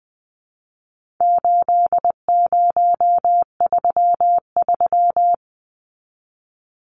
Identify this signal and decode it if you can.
Morse code, heard as a keyed tone, about 20 wpm, 700 Hz. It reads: 8033
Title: Numbers - 4-Digit 20wpm